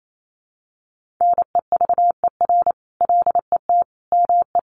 Morse code, heard as a keyed tone, about 28 wpm, 700 Hz. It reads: NEVER LET ME
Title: Sentences from Top 1500 Words 28wpm